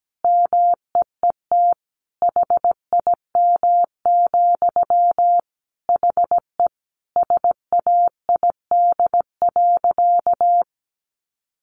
Morse code, heard as a keyed tone, about 17 wpm, 700 Hz. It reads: MEET HIM, HE SAID.
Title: Sentences from Top 1100 Words 17wpm